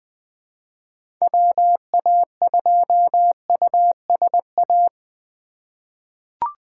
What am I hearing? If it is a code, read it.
Morse code, heard as a keyed tone, about 20 wpm, 700 Hz. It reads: WA2USA
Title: 2023 SOTA Chaser Call Signs - 20wpm